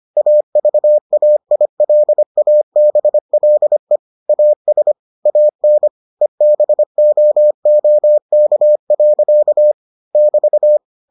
Morse code, heard as a keyed tone, about 25 wpm, 600 Hz. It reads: AVAILABLE AS AN EBOOK. <BT>